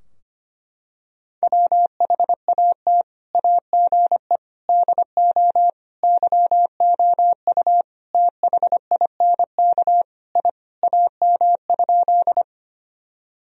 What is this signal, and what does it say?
Morse code, heard as a keyed tone, about 25 wpm, 700 Hz. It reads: WHAT AGE DO YOU THINK I AM?
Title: Sentences from Top 1000 Words - Review Part 3 - 25wpm